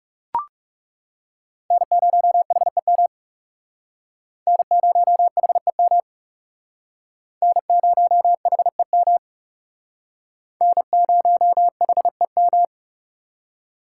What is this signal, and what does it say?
Morse code, heard as a keyed tone, about 37 wpm, 700 Hz. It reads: N0HEM N0HEM N0HEM N0HEM
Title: Speed Racing - USA 1x3 Call Signs 30wpm